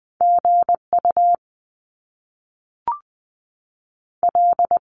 Morse code, heard as a keyed tone, about 20 wpm, 700 Hz. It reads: G U L
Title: Top Two Letter Combinations in Words 20wpm